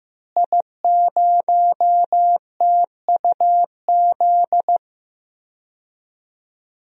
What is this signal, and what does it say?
Morse code, heard as a keyed tone, about 15 wpm, 700 Hz. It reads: I0TUZ
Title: DXCC-100 Call Signs - 5 Characters - Repeated 3x - 15wpm